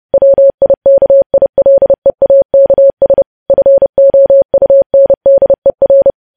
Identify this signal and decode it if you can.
Morse code, heard as a keyed tone, about 30 wpm, 550 Hz. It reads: WIKILEAKS FOUNDER